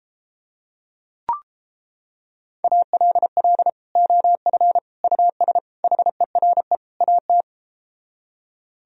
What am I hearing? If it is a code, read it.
Morse code, heard as a keyed tone, about 33 wpm, 700 Hz. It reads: ALL OF US HERE AT